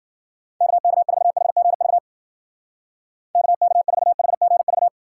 Morse code, heard as a keyed tone, about 52 wpm, 700 Hz. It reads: KK4HGV KK4HGV